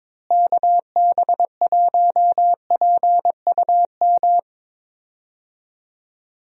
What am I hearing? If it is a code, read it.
Morse code, heard as a keyed tone, about 23 wpm, 700 Hz. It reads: KB1PUM
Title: USA Call Signs - 6 Characters - Repeated 2x - 22wpm